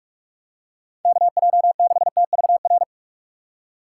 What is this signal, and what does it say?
Morse code, heard as a keyed tone, about 44 wpm, 700 Hz. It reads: K J 6 T V R
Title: USA Call Signs - 6 Characters - 6x Character Spacing - 45wpm